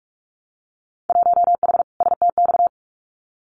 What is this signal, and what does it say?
Morse code, heard as a keyed tone, about 46 wpm, 700 Hz. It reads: 1H STX
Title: Winter Field Day Exchanges 45wpm